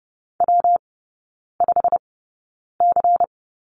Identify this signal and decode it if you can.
Morse code, heard as a keyed tone, about 30 wpm, spaced as for 9 wpm, 700 Hz. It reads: W5C
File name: Lesson 19e - Letter w - Three Characters 30wpm